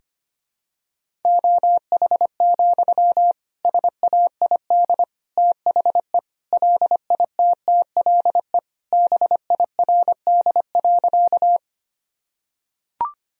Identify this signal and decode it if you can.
Morse code, heard as a keyed tone, about 25 wpm, 700 Hz. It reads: OH, SAID THE LITTLE BIRD.
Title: Sentences from Top 1000 Words - Review Part 2 - 25wpm